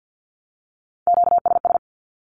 Morse code, heard as a keyed tone, about 48 wpm, 700 Hz. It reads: QSS
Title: Q-Codes Repeated 3x 50wpm